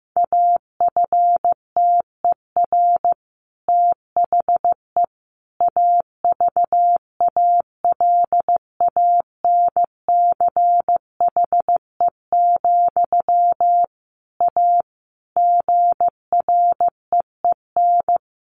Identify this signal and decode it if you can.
Morse code, heard as a keyed tone, about 15 wpm, 700 Hz. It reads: AFTER THE AVALANCHE, A GREEN
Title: A Sound Of Thunder 15wpm